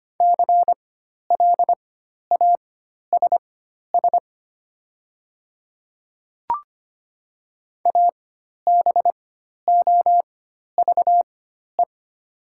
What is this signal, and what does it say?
Morse code, heard as a keyed tone, about 25 wpm, spaced as for 12 wpm, 700 Hz. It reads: CLASS ABOVE
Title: 4x Character Spacing - Top 500 Words - 25wpm